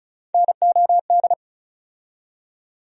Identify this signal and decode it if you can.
Morse code, heard as a keyed tone, about 36 wpm, 700 Hz. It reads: NOD
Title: USA Call Sign Postfixes - 3 Letters 35wpm